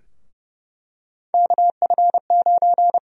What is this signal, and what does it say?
Morse code, heard as a keyed tone, about 30 wpm, 700 Hz. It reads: KF9
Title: USA Call Sign Prefixes - Repeated 2x - 30wpm